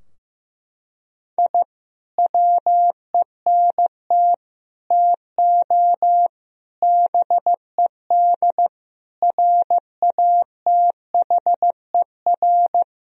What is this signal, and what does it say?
Morse code, heard as a keyed tone, about 15 wpm, 700 Hz. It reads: I WENT TO BED RATHER
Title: Sentences from Top 1000 Words - Review Part 4 - 15wpm